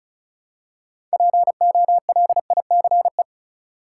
Morse code, heard as a keyed tone, about 36 wpm, 700 Hz. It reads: POLICE
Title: Six Letter Words - 35wpm